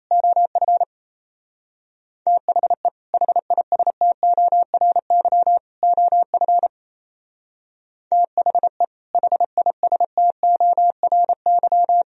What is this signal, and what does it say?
Morse code, heard as a keyed tone, about 36 wpm, 700 Hz. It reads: OF THE HISTORY OF THE HISTORY